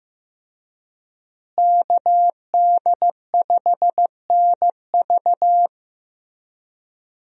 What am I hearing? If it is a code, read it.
Morse code, heard as a keyed tone, about 15 wpm, 700 Hz. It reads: KD5NV
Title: USA Call Signs - 5 Characters - Repeated 3x - 15wpm